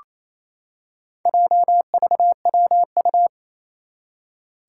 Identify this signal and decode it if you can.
Morse code, heard as a keyed tone, about 28 wpm, 700 Hz. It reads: JVWU
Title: Four Challenging Letter-Number Combinations 28wpm